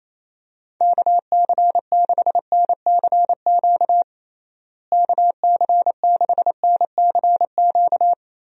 Morse code, heard as a keyed tone, about 28 wpm, 700 Hz. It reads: KC6NCQ KC6NCQ
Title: USA Call Signs - 6 Characters - Repeated 2x - 28wpm